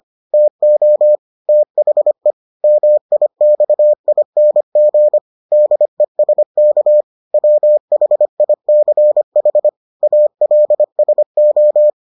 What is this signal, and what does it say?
Morse code, heard as a keyed tone, about 25 wpm, 600 Hz. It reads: TO THE MIXING DESK WHICH ALSO